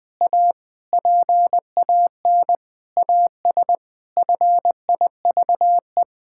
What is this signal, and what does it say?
Morse code, heard as a keyed tone, about 20 wpm, 700 Hz. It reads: A PAN AS FIVE
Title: Pangram Challenge v2 - 20wpm